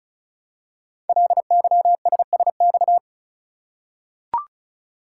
Morse code, heard as a keyed tone, about 36 wpm, 690 Hz. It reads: LYSSX
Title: Code Groups - 5 Letters - 35wpm